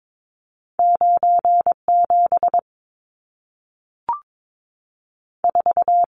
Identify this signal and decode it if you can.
Morse code, heard as a keyed tone, about 22 wpm, 700 Hz. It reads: 9 7 4
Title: Numbers - 2-Digit 22wpm